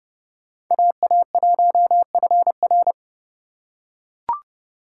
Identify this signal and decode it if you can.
Morse code, heard as a keyed tone, about 30 wpm, 700 Hz. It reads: AA1FR
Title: USA Call Signs - 5 Characters - 4x Character Spacing - 30wpm